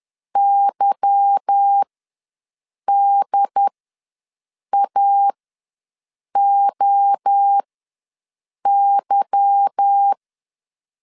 Morse code, heard as a keyed tone, about 10 wpm, spaced as for 6 wpm, 800 Hz. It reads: YDAOY